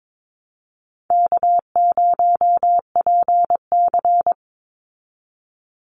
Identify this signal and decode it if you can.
Morse code, heard as a keyed tone, about 22 wpm, 700 Hz. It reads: K0PC
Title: Easy Call Signs Repeated 2x - 22wpm